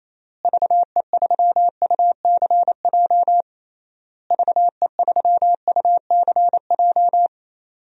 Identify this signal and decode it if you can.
Morse code, heard as a keyed tone, about 28 wpm, 700 Hz. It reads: VE3UCJ VE3UCJ
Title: DXCC-100 Call Signs - 6 Characters - Repeated 3x - 28wpm